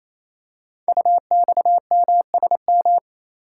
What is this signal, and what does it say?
Morse code, heard as a keyed tone, about 28 wpm, 700 Hz. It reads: UXMSM
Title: Code Groups - 5 Letters - 28wpm